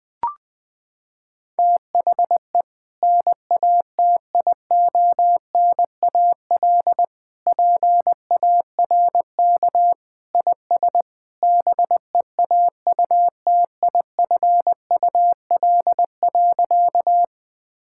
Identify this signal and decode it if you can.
Morse code, heard as a keyed tone, about 21 wpm, 700 Hz. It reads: THE NATIONAL PARK IS BEAUTIFUL.
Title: Sentences with 2-Word Phrases - 20wpm